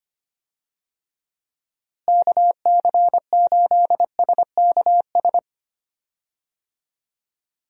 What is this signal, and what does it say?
Morse code, heard as a keyed tone, about 25 wpm, 700 Hz. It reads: KC8SKS
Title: USA Call Signs - 6 Characters - Repeated 3x - 25wpm